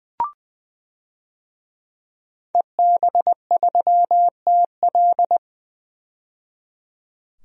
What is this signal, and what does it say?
Morse code, heard as a keyed tone, about 21 wpm, 700 Hz. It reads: EB3TL
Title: DXCC-100 Call Signs - 5 Characters - 20wpm